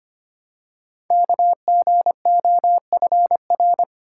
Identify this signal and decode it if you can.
Morse code, heard as a keyed tone, about 25 wpm, 700 Hz. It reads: KGOFR
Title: Code Groups - 5 Letters - 25wpm